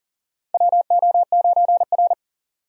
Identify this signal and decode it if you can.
Morse code, heard as a keyed tone, about 40 wpm, 700 Hz. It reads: WO9R
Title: USA Call Signs - 4 Characters - 40wpm